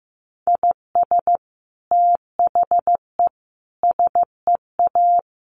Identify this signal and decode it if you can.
Morse code, heard as a keyed tone, about 15 wpm, 700 Hz. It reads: IS THE SEA
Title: Sentences from Top 1400 Words 15wpm